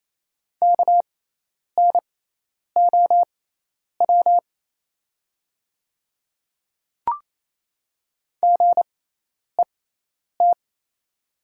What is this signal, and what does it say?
Morse code, heard as a keyed tone, about 28 wpm, spaced as for 10 wpm, 700 Hz. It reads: KNOW GET
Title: 6x Character Spacing - Top 200 Words - 28wpm